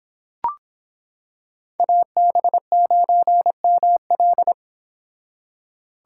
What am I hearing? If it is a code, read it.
Morse code, heard as a keyed tone, about 26 wpm, 700 Hz. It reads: AB9ML